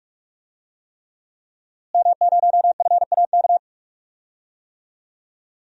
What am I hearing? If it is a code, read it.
Morse code, heard as a keyed tone, about 41 wpm, 700 Hz. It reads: M0FAK
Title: DXCC-100 Call Signs - 5 Characters - Repeated 3x - 45wpm